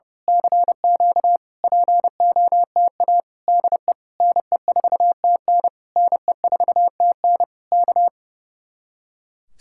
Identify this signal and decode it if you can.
Morse code, heard as a keyed tone, about 28 wpm, 700 Hz. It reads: CQ POTA DE NE4TN NE4TN K
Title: Speed-Racing Common QSO Phrases Encore - 30wpm